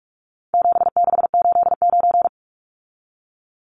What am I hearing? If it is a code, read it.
Morse code, heard as a keyed tone, about 44 wpm, 700 Hz. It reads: 7689
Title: Numbers - 4-Digit 45wpm